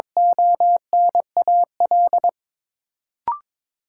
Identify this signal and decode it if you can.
Morse code, heard as a keyed tone, about 22 wpm, 700 Hz. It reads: ONAL
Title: Sentences from Top 1200 Words - New Word Drill Alt - 22wpm